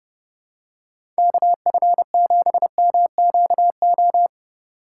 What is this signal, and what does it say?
Morse code, heard as a keyed tone, about 30 wpm, 700 Hz. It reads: KF7MQO
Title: USA Call Signs - 6 Characters - 2x Character Spacing - 30wpm